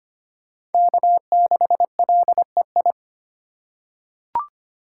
Morse code, heard as a keyed tone, about 25 wpm, 700 Hz. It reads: K6LEI
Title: USA Call Signs - 5 Characters - 3x Character Spacing - 25wpm